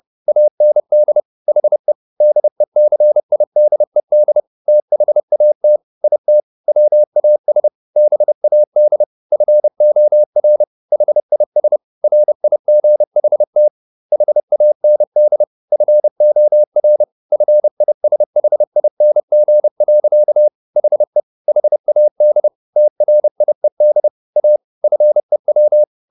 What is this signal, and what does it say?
Morse code, heard as a keyed tone, about 30 wpm, 600 Hz. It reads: AND HE DECIDED THAT IT WAS BAD FOR HIS RIGHT HAND FOR FISHING. HE HAD TRIED A FEW
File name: the old man and the sea 2 30wpm